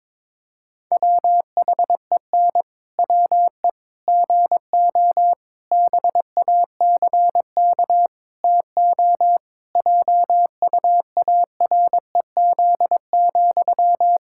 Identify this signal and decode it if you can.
Morse code, heard as a keyed tone, about 22 wpm, 700 Hz. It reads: WHEN WE GO BACK TO JUAREZ,
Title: Pangram Challenge v2 - 22wpm